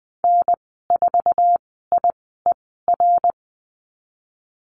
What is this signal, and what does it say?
Morse code, heard as a keyed tone, about 20 wpm, spaced as for 14 wpm, 700 Hz. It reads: N4IER
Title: 062 Lesson 11g - Letter h - Call Signs 20wpm